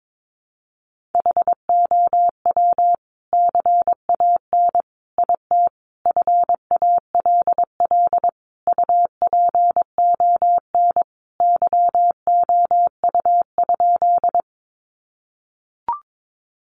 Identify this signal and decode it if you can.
Morse code, heard as a keyed tone, about 22 wpm, 700 Hz. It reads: HOW CAN IT FALL UPON YOU?
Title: Sentences from Top 1100 Words 22wpm